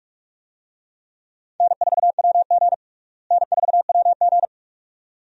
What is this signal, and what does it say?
Morse code, heard as a keyed tone, about 42 wpm, 700 Hz. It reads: N4WG N4WG
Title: USA Call Signs - 4 Characters - Repeated 2x - 45wpm